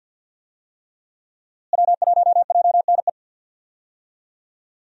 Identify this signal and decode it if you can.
Morse code, heard as a keyed tone, about 51 wpm, 700 Hz. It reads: W1JNE
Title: USA Call Signs - 5 Characters - Repeated 3x - 50wpm